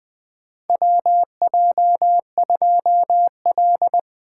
Morse code, heard as a keyed tone, about 20 wpm, 700 Hz. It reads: WJ2L
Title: K1USN SST QSO Practice - 20wpm